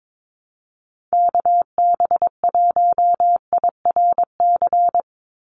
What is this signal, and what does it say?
Morse code, heard as a keyed tone, about 22 wpm, 700 Hz. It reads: KB1IRC
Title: USA Call Signs 22wpm